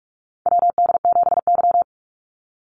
Morse code, heard as a keyed tone, about 46 wpm, 690 Hz. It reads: WD7Y
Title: NA SOTA Chaser Callsigns 45wpm